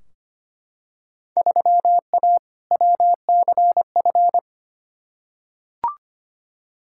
Morse code, heard as a keyed tone, about 25 wpm, 700 Hz. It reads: 3A WCF
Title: Field Day Exchanges Encore - 25wpm